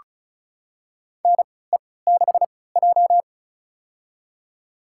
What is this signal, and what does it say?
Morse code, heard as a keyed tone, about 36 wpm, spaced as for 20 wpm, 700 Hz. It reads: NE6J
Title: USA Call Signs - 3x Character Spacing - 35wpm